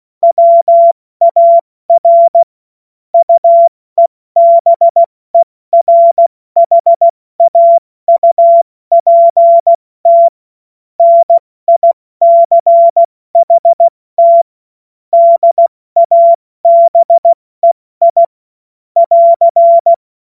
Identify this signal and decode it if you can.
Morse code, heard as a keyed tone, about 16 wpm, 680 Hz. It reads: WAR UEBERHAUPT NICHT DABEI <AR>
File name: Morsekurs/Witze/Witz 12 Dezember 2023 70 F80